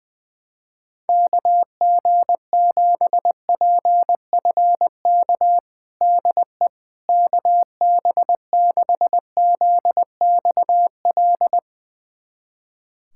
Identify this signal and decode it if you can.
Morse code, heard as a keyed tone, about 20 wpm, 700 Hz. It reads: KG7PFK DE KB6ZXL
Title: USA Callsign DE Callsign - 6 Characters - 20wpm